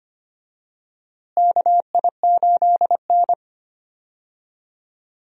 Morse code, heard as a keyed tone, about 25 wpm, 700 Hz. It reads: KI8N
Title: USA Call Signs - 4 Characters - Repeated 3x - 25wpm